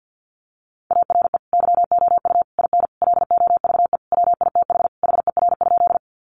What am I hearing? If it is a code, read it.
Morse code, heard as a keyed tone, about 50 wpm, 700 Hz. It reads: ARE YOU IN LOVE WITH HER?
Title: Sentences from Top 400 Words 50wpm